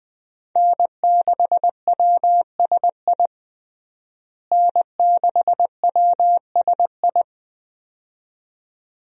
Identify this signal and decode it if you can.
Morse code, heard as a keyed tone, about 20 wpm, 700 Hz. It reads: N6WSI N6WSI
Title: USA Call Signs - 5 Characters - Repeated 2x - 20wpm